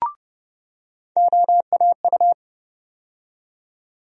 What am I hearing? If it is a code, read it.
Morse code, heard as a keyed tone, about 30 wpm, 700 Hz. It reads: OAU
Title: USA Call Sign Postfixes - 3 Letters 30wpm